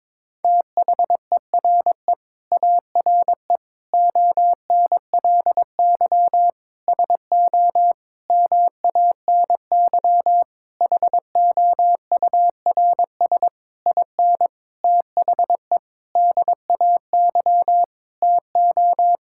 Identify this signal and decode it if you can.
Morse code, heard as a keyed tone, about 21 wpm, 700 Hz. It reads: THERE ARE ONLY SO MANY HOURS IN THE DAY TO
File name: Sentences with 9-Word Phrases - 22wpm